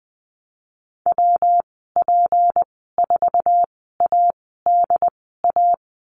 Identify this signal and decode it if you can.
Morse code, heard as a keyed tone, about 20 wpm, spaced as for 15 wpm, 700 Hz. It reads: WP4ADA
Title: 167 Lesson 27g - Number 7 - Call Signs 20wpm